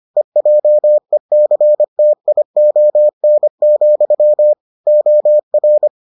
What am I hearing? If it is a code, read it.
Morse code, heard as a keyed tone, about 25 wpm, 600 Hz. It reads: EJECTION, OR